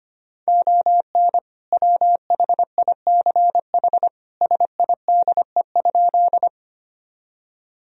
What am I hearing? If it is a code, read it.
Morse code, heard as a keyed tone, about 25 wpm, 700 Hz. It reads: ON WHICH SIDE?
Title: Sentences from Top 1000 Words - Review Part 1 - 25wpm